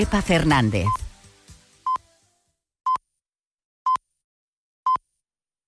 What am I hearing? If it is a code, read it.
Radio time signal (time pips), with speech. It is heard from 0.9 to 5.0 s.